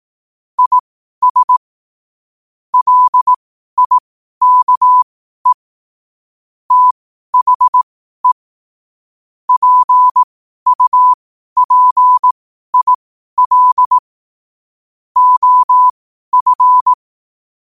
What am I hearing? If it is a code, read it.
Morse code, heard as a keyed tone, about 18 wpm, spaced as for 12 wpm, 1000 Hz. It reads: IS LIKE THE PUPIL OF
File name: qotdfiles / 12wpm